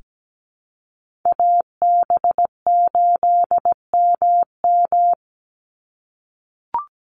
Morse code, heard as a keyed tone, about 17 wpm, 700 Hz. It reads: AB8MM
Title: USA Call Signs 17wpm